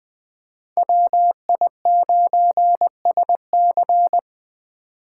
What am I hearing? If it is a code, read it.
Morse code, heard as a keyed tone, about 21 wpm, 700 Hz. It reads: WI9SC
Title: Speed Racing - USA Call Signs 17wpm